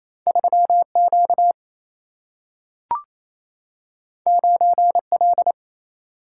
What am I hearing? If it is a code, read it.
Morse code, heard as a keyed tone, about 28 wpm, 700 Hz. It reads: 3 Q 9 L
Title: Single-Letter Single-Number 28wpm